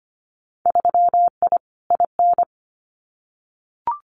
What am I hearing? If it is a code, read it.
Morse code, heard as a keyed tone, about 25 wpm, 700 Hz. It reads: 3I IN
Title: Winter Field Day Exchanges 25wpm